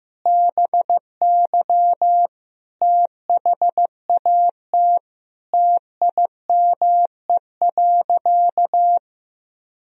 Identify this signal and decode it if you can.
Morse code, heard as a keyed tone, about 15 wpm, 700 Hz. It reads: BY THAT TIME.
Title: Sentences from Top 1000 Words - Review Part 2 - 15wpm